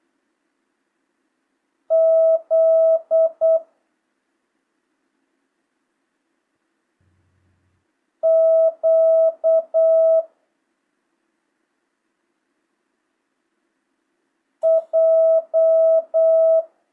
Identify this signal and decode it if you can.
Morse code, heard as a keyed tone, about 8 wpm, 650 Hz. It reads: Z Q J
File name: jh8cbh / 2025cw